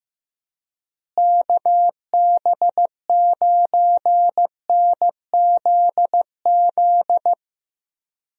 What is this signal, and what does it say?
Morse code, heard as a keyed tone, about 15 wpm, 700 Hz. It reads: KB9NZZ
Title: USA Call Signs - 6 Characters - Repeated 2x - 15wpm